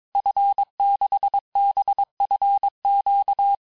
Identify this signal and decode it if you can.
Morse code, heard as a keyed tone, about 22 wpm, 800 Hz. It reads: F6BFQ
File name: cours-du-mercredi / audios / cours-nr-860-FIN-de-QSOr-vitesse-21MPM